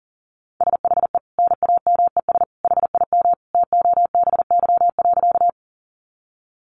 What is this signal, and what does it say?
Morse code, heard as a keyed tone, about 37 wpm, 700 Hz. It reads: SHE NAMES HIM TOBY.
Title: Easy Phrases 40wpm